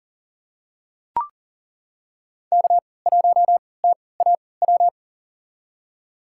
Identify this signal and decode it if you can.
Morse code, heard as a keyed tone, about 38 wpm, spaced as for 22 wpm, 700 Hz. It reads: K1TAW
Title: USA Call Signs - 3x Character Spacing - 40wpm